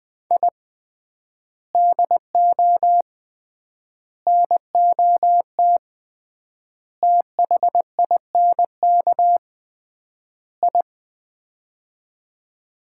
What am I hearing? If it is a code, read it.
Morse code, heard as a keyed tone, about 20 wpm, 700 Hz. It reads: I DO NOT THINK I
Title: 3x Word-Spacing - Sets of 5 Words - 20wpm